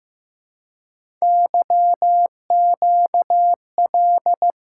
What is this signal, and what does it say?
Morse code, heard as a keyed tone, about 15 wpm, 700 Hz. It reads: YQL
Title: Code Groups - 5 Letters - 15wpm